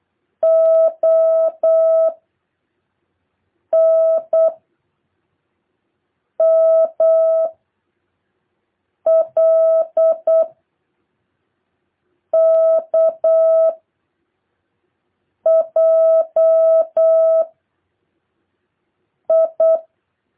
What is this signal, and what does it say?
Morse code, heard as a keyed tone, about 8 wpm, 660 Hz. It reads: O N M L K J I